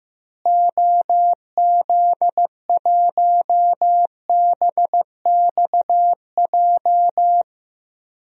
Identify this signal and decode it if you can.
Morse code, heard as a keyed tone, about 15 wpm, 700 Hz. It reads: OZ1BXJ
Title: DXCC-100 Call Signs - 6 Characters - Repeated 3x - 15wpm